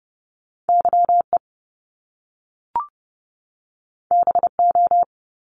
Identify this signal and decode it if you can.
Morse code, heard as a keyed tone, about 30 wpm, 700 Hz. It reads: YE BO
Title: Top Two Letter Combinations in Words 30wpm